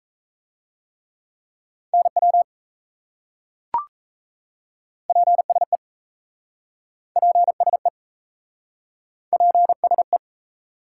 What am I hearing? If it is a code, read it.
Morse code, heard as a keyed tone, about 37 wpm, 700 Hz. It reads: NW PSE PSE PSE